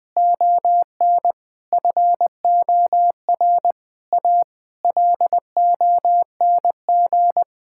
Morse code, heard as a keyed tone, about 20 wpm, 700 Hz. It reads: ON FOR A LONG
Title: Sets of 8 Words - 20wpm